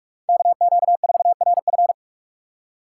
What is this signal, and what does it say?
Morse code, heard as a keyed tone, about 46 wpm, 700 Hz. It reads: KQ4RF
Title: USA Call Signs - 5 Characters - 3x Character Spacing - 45wpm